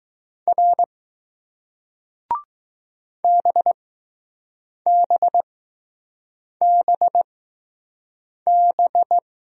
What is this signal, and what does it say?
Morse code, heard as a keyed tone, about 20 wpm, spaced as for 6 wpm, 700 Hz. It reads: R BBBB